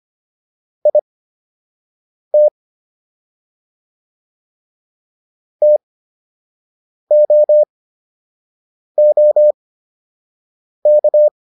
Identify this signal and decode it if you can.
Morse code, heard as a keyed tone, about 25 wpm, spaced as for 6 wpm, 600 Hz. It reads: IT TOOK